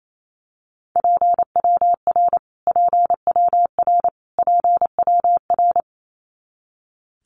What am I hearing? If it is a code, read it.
Morse code, heard as a keyed tone, about 28 wpm, 700 Hz. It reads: PWR PWR PWR
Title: Abbreviations Repeated 3x - 28wpm